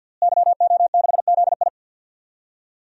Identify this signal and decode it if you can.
Morse code, heard as a keyed tone, about 52 wpm, 700 Hz. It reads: YO6ZI
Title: DXCC-100 Call Signs - 5 Characters - Repeated 2x - 50wpm